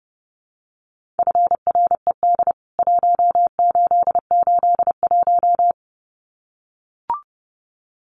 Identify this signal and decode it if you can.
Morse code, heard as a keyed tone, about 30 wpm, 700 Hz. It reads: FRED 1881
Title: CWT Exchange 30wpm